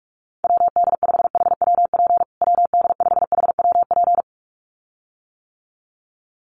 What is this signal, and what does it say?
Morse code, heard as a keyed tone, about 44 wpm, 700 Hz. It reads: WD5HWP WD5HWP
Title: USA Call Signs Repeated 2x 45wpm